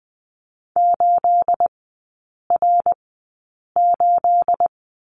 Morse code, heard as a keyed tone, about 20 wpm, spaced as for 8 wpm, 690 Hz. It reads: 8R8
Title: 212 Lesson 34e - Number 8 - Three Characters 20wpm